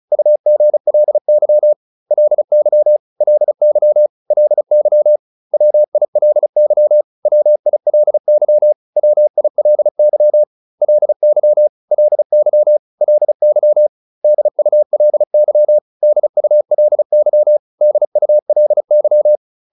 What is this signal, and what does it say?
Morse code, heard as a keyed tone, about 34 wpm, 600 Hz. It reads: UGLY LY LY LY WILY WILY WILY LY LY LY DULY DULY DULY